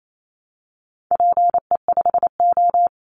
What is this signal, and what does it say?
Morse code, heard as a keyed tone, about 28 wpm, 700 Hz. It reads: PE5O
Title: Call Signs Repeated 3x - 28wpm